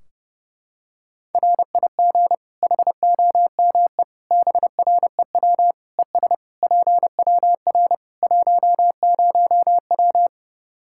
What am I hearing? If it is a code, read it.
Morse code, heard as a keyed tone, about 30 wpm, 700 Hz. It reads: RIG HOME BREW ES PWR 10W
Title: Practice QSO - Exchange 2 - Info - 30wpm